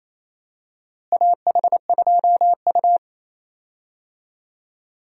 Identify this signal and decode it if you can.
Morse code, heard as a keyed tone, about 28 wpm, 700 Hz. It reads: AH2U
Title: USA Call Signs - 4 Characters - Repeated 3x - 28wpm